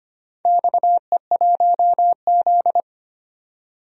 Morse code, heard as a keyed tone, about 25 wpm, 700 Hz. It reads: XE1Z
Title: DXCC-100 Call Signs - 4 Characters - 25wpm